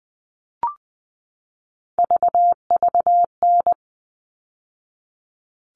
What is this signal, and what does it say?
Morse code, heard as a keyed tone, about 20 wpm, 700 Hz. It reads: VVN
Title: All Cut Numbers - 3-Digit 20wpm